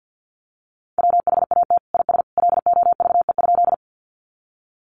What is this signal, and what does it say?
Morse code, heard as a keyed tone, about 50 wpm, 700 Hz. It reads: WHAT IS LOVE?
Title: Sentences from Top 400 Words 50wpm